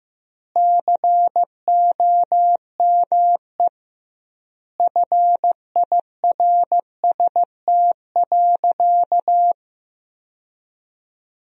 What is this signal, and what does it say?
Morse code, heard as a keyed tone, about 15 wpm, 700 Hz. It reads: COME FIRST.
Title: 2x Word-Spacing - Sentences from Top 300 Words - 15wpm